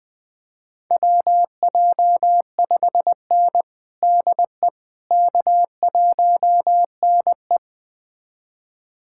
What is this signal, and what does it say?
Morse code, heard as a keyed tone, about 21 wpm, 700 Hz. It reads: WJ5N DE K1NE
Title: USA Callsign DE Callsign - 4 Characters - 20wpm